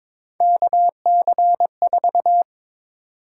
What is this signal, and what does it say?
Morse code, heard as a keyed tone, about 22 wpm, 700 Hz. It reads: KC4
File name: practice / 22wpm / Three Letter-Number Combinations - 22wpm